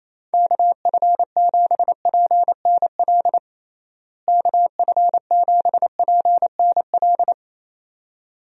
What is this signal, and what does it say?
Morse code, heard as a keyed tone, about 28 wpm, 700 Hz. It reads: KF7PNL KF7PNL
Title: USA Call Signs - 6 Characters - Repeated 2x - 28wpm